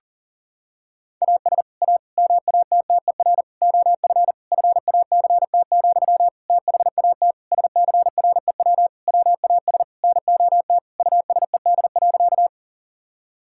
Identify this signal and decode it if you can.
Morse code, heard as a keyed tone, about 40 wpm, 700 Hz. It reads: AS A MATTER OF FACT, THAT SCREW WAS NOT USED.
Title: Sentences from Top 1600 Words 40wpm